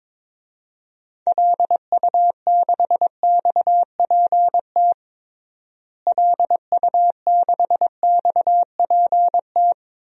Morse code, heard as a keyed tone, about 21 wpm, 700 Hz. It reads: LU6XPT LU6XPT
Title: DXCC-100 Call Signs - 6 Characters - Repeated 2x - 22wpm